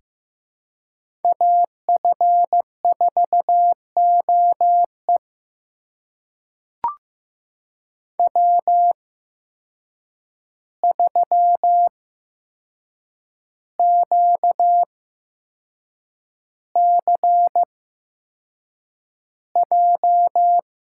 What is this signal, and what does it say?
Morse code, heard as a keyed tone, about 15 wpm, spaced as for 6 wpm, 700 Hz. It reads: AF4OE W3QCJ
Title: USA Call Signs - 5 Characters - 8x Character Spacing - 15wpm